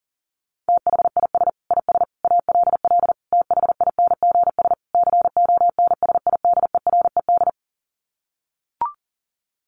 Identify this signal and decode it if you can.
Morse code, heard as a keyed tone, about 40 wpm, 700 Hz. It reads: THIS IS ALL THINGS CONSIDERED
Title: Sets of 5 Words 40wpm